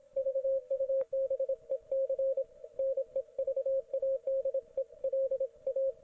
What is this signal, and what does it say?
Morse code, heard as a keyed tone, about 27 wpm, 540 Hz, with radio noise. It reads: VUBEC NEVADELA